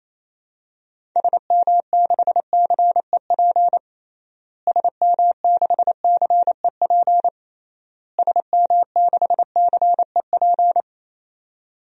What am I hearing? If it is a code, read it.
Morse code, heard as a keyed tone, about 28 wpm, 700 Hz. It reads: SM6CEP SM6CEP SM6CEP
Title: DXCC-100 Call Signs - 6 Characters - Repeated 3x - 28wpm